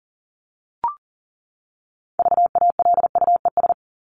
Morse code, heard as a keyed tone, about 40 wpm, 700 Hz. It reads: VALUES
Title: ES:S Suffix - Top 100 Words with Base Words 40wpm